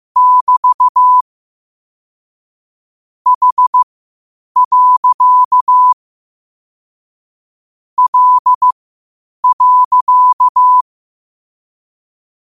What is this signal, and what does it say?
Morse code, heard as a keyed tone, about 15 wpm, spaced as for 8 wpm, 1000 Hz. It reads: <BT> H. L.
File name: qotdfiles / 8wpm